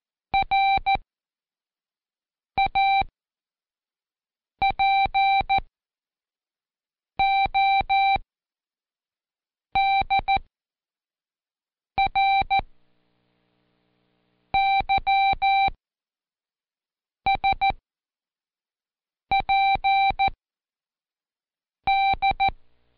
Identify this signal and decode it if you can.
Morse code, heard as a keyed tone, about 14 wpm, spaced as for 5 wpm, 780 Hz. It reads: RAPODRYSPD
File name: Seg29-Day4-CheckTest 4